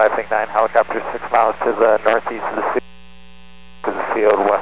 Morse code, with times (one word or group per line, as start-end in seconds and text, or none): none